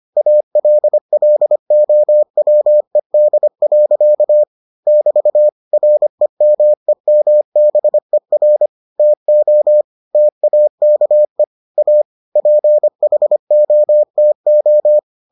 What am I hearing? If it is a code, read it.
Morse code, heard as a keyed tone, about 25 wpm, 600 Hz. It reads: ALLOWED. <BT> REMEMBER TO TAKE A PHOTO